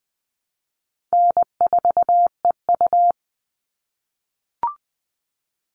Morse code, heard as a keyed tone, about 20 wpm, 700 Hz. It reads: N4EU
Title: All Cut Numbers - 4-Digit 20wpm